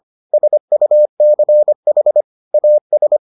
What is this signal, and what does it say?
Morse code, heard as a keyed tone, about 25 wpm, 600 Hz. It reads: SUCH AS